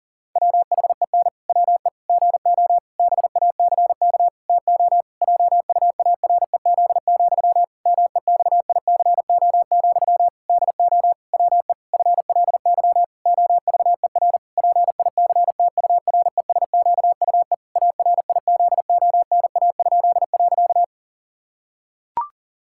Morse code, heard as a keyed tone, about 40 wpm, 700 Hz. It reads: WHEN WE GO BACK TO JUAREZ, MEXICO, DO WE FLY OVER PICTURESQUE ARIZONA?.
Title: Pangram Challenge v2 - 40wpm